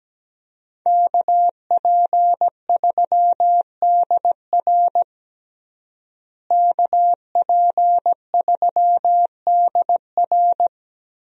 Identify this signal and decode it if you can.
Morse code, heard as a keyed tone, about 17 wpm, 700 Hz. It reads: KP3DR KP3DR
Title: USA Call Signs - 5 Characters - Repeated 2x - 17wpm